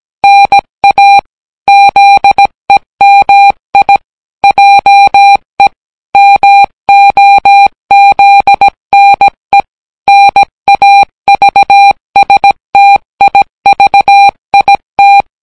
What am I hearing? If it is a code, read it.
Morse code, heard as a keyed tone, about 17 wpm, 800 Hz. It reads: NA ZEMI JE MOZNE NAVSTIVIT